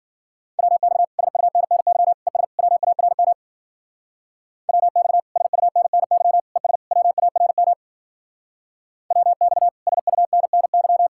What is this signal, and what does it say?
Morse code, heard as a keyed tone, about 50 wpm, 700 Hz. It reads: WX SUNNY ES WARM WX SUNNY ES WARM WX SUNNY